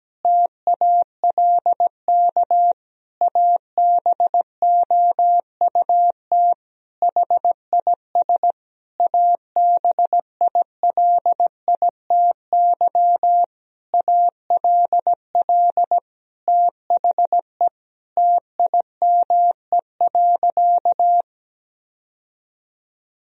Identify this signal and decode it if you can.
Morse code, heard as a keyed tone, about 17 wpm, 700 Hz. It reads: TALK ABOUT HIS ABILITY ALL THE TIME.
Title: Sentences from Top 1000 Words - Review Part 3 - 17wpm